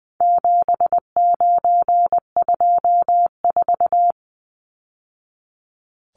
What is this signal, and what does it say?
Morse code, heard as a keyed tone, about 20 wpm, 700 Hz. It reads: 7924
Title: Numbers - 4-Digit 20wpm